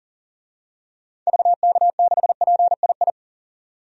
Morse code, heard as a keyed tone, about 37 wpm, 700 Hz. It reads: VK6PII
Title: DXCC-100 Call Signs - 6 Characters - 40wpm